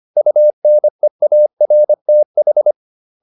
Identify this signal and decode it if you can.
Morse code, heard as a keyed tone, about 25 wpm, 600 Hz. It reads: UNEARTH